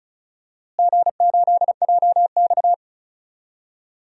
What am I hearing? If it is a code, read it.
Morse code, heard as a keyed tone, about 36 wpm, 700 Hz. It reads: G8JX
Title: DXCC-100 Call Signs - 4 Characters - Repeated 2x - 35wpm